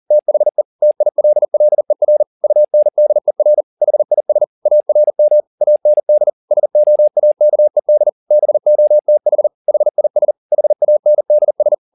Morse code, heard as a keyed tone, about 40 wpm, 600 Hz. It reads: THE TILLER UNDER HIS ARM AND SOAKED BOTH HIS HANDS
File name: the old man and the sea 2 40wpm